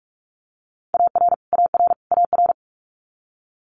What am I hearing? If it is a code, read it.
Morse code, heard as a keyed tone, about 46 wpm, 690 Hz. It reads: AR AR AR
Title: Multi-letter Phonemes Repeated 3x 45wpm